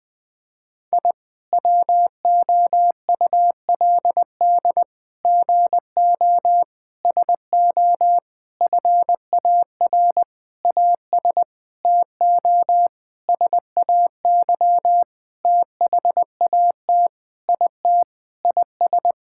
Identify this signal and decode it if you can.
Morse code, heard as a keyed tone, about 20 wpm, 700 Hz. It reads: I WOULD GO SO FAR AS TO SAY THAT IT IS
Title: Sentences with 9-Word Phrases - 20wpm